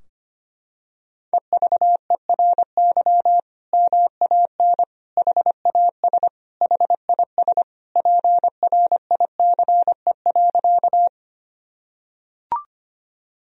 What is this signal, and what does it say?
Morse code, heard as a keyed tone, about 25 wpm, 700 Hz. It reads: EVERY MAN HAS HIS PRICE.
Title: Sentences from Top 1000 Words - Review Part 4 - 25wpm